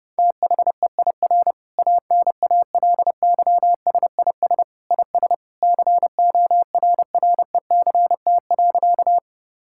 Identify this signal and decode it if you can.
Morse code, heard as a keyed tone, about 30 wpm, 700 Hz. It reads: THEIR ANALYSIS IS CORRECT.
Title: Sentences from Top 1000 Words - Review Part 5 - 30wpm